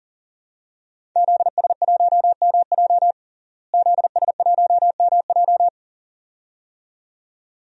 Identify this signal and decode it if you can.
Morse code, heard as a keyed tone, about 37 wpm, 700 Hz. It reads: ZS1MJ ZS1MJ
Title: DXCC-100 Call Signs - 5 Characters - Repeated 2x - 40wpm